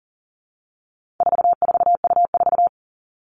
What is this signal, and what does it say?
Morse code, heard as a keyed tone, about 40 wpm, 700 Hz. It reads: 44U4
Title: All Cut Numbers - 4-Digit 40wpm